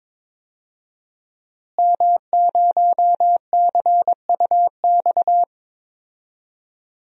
Morse code, heard as a keyed tone, about 22 wpm, 700 Hz. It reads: M0CUX
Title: DXCC-100 Call Signs - 5 Characters - Repeated 3x - 22wpm